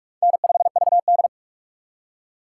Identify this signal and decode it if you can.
Morse code, heard as a keyed tone, about 48 wpm, 700 Hz. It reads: N5VD
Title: USA Call Signs - 4 Characters - 2x Character Spacing - 45wpm